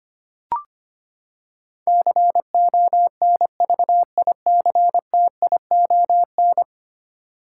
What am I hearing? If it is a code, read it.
Morse code, heard as a keyed tone, about 25 wpm, 700 Hz. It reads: CONVICTION
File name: Ten Letter Words - 25wpm